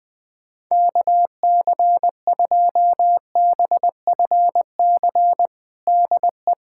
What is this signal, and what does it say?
Morse code, heard as a keyed tone, about 20 wpm, 700 Hz. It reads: KC2BFC DE
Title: USA Callsign DE Callsign - 6 Characters - 20wpm